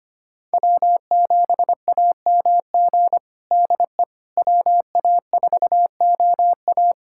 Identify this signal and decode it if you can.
Morse code, heard as a keyed tone, about 25 wpm, 700 Hz. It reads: W7AMG DE WA4OA
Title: USA Callsign DE Callsign - 5 Characters - 25wpm